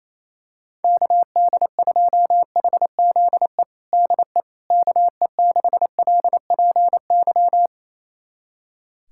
Morse code, heard as a keyed tone, about 27 wpm, 700 Hz. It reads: KD2HZE DE KE6LPY
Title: USA Callsign DE Callsign - 6 Characters - 28wpm